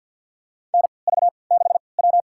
Morse code, heard as a keyed tone, about 47 wpm, spaced as for 28 wpm, 700 Hz. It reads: NV6W
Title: USA Call Signs - 4 Characters - 3x Character Spacing - 50wpm